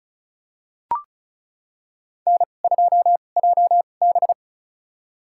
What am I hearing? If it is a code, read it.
Morse code, heard as a keyed tone, about 36 wpm, spaced as for 25 wpm, 700 Hz. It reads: N2JB
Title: USA Call Signs - 2x Character Spacing - 35wpm